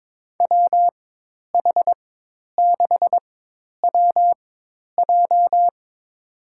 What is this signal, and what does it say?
Morse code, heard as a keyed tone, about 22 wpm, spaced as for 10 wpm, 700 Hz. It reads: WH6WJ
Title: USA Call Signs - 5 Characters - 4x Character Spacing - 22wpm